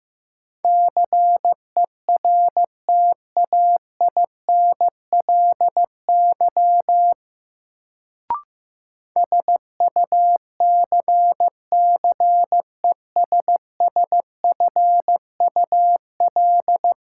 Morse code, heard as a keyed tone, about 16 wpm, 700 Hz. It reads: CERTAINLY SUCCESSFUL
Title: Sentences from Top 900 Words - New Word Drill - 15wpm